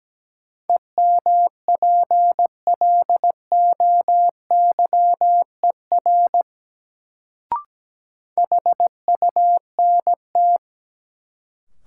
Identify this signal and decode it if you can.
Morse code, heard as a keyed tone, about 17 wpm, 700 Hz. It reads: EMPLOYER HUNT
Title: Sentences from Top 1600 Words - New Word Drill - 17wpm